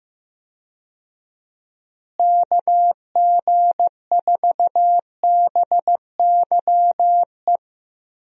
Morse code, heard as a keyed tone, about 15 wpm, 700 Hz. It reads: KG4BYE
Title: USA Call Signs - 6 Characters - Repeated 3x - 15wpm